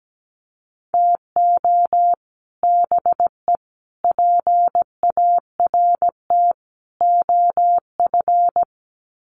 Morse code, heard as a keyed tone, about 17 wpm, 700 Hz. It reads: TO BE PART OF
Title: Sets of 4 Words 17wpm